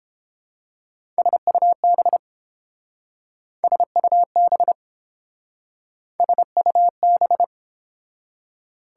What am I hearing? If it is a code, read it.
Morse code, heard as a keyed tone, about 30 wpm, 700 Hz. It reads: SUB SUB SUB